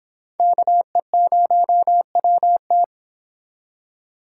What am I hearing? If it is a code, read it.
Morse code, heard as a keyed tone, about 30 wpm, 700 Hz. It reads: KE0WT